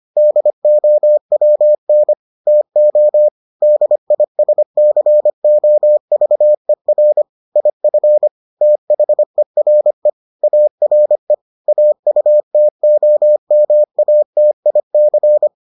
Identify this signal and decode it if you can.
Morse code, heard as a keyed tone, about 25 wpm, 600 Hz. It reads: DOWN TO DISCOVER IF THERE ARE AUTOMATIC